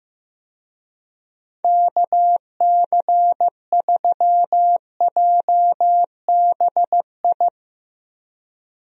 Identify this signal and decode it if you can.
Morse code, heard as a keyed tone, about 15 wpm, 700 Hz. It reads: KC3JBI
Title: USA Call Signs - 6 Characters - 15wpm